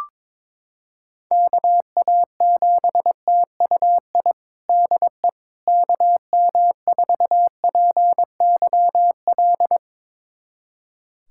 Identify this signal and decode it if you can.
Morse code, heard as a keyed tone, about 22 wpm, 700 Hz. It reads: KA7TUI DE KM4PYL
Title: USA Callsign DE Callsign 22wpm